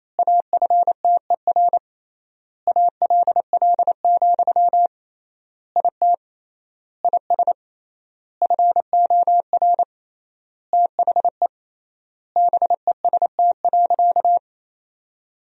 Morse code, heard as a keyed tone, about 28 wpm, 700 Hz. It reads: AFTER ALL, IT IS FOR THE BEST.
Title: 3x Word-Spacing - Sentences from Top 500 Words - 28wpm